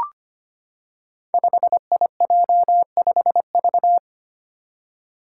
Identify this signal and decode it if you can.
Morse code, heard as a keyed tone, about 25 wpm, 700 Hz. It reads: 5IJ5V
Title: Five Challenging Letter-Number Combinations 25wpm